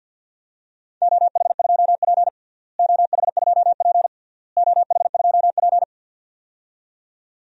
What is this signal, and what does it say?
Morse code, heard as a keyed tone, about 51 wpm, 700 Hz. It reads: OH2P OH2P OH2P
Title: DXCC-100 Call Signs - 4 Characters - Repeated 3x - 50wpm